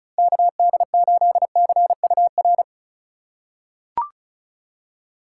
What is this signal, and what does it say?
Morse code, heard as a keyed tone, about 36 wpm, 700 Hz. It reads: KD8CUR
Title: USA Call Signs - 6 Characters - 4x Character Spacing - 35wpm